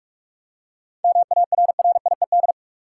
Morse code, heard as a keyed tone, about 42 wpm, 700 Hz. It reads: MARRIED
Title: Seven Letter Words - 45wpm